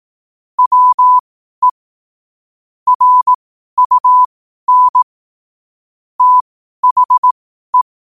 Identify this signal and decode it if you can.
Morse code, heard as a keyed tone, about 18 wpm, spaced as for 12 wpm, 1000 Hz. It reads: WE RUN THE
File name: qotdfiles / 12wpm